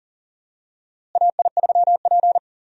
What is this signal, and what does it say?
Morse code, heard as a keyed tone, about 40 wpm, 700 Hz. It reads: AI3P
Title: USA Call Signs - 4 Characters - Repeated 3x - 40wpm